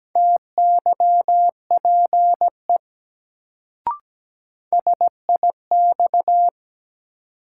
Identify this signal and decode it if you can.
Morse code, heard as a keyed tone, about 17 wpm, 700 Hz. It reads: TYPE SIX
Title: Sentences from Top 600 Words - New Word Drill - 17wpm